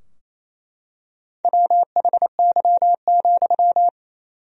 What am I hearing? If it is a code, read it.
Morse code, heard as a keyed tone, about 27 wpm, 700 Hz. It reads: WHY,
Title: 4x Word-Spacing - Sentences from Top 200 Words - 28wpm